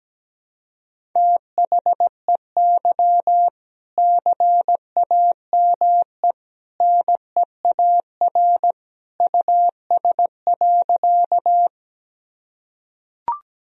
Morse code, heard as a keyed tone, about 17 wpm, 700 Hz. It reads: THEY CAME NEAR US.
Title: Sentences from Top 1000 Words - Review Part 1 - 17wpm